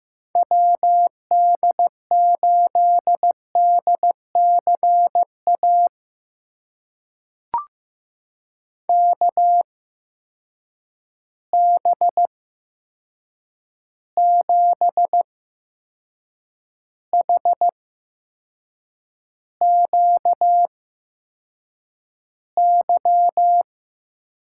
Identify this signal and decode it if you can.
Morse code, heard as a keyed tone, about 15 wpm, spaced as for 6 wpm, 700 Hz. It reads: WD8DCA KB7HQY
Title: USA Call Signs - 6 Characters - 8x Character Spacing - 15wpm